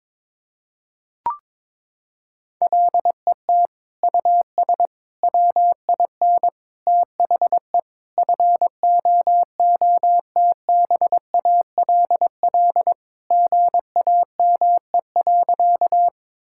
Morse code, heard as a keyed tone, about 22 wpm, 700 Hz. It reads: LET US WIN THE FOOTBALL GAME.
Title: Sentences from Top 1600 Words 22wpm